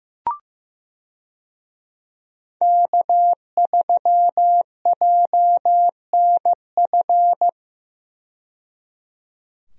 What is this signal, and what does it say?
Morse code, heard as a keyed tone, about 15 wpm, 700 Hz. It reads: K3JNF
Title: USA Call Signs - 5 Characters - 15wpm